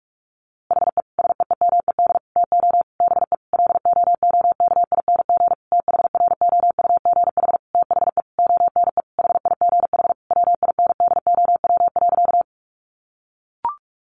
Morse code, heard as a keyed tone, about 44 wpm, 700 Hz. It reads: HE SEEMED TO BE LOOKING THROUGH THE ONE HIGH WINDOW.
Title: A Sound Of Thunder 45wpm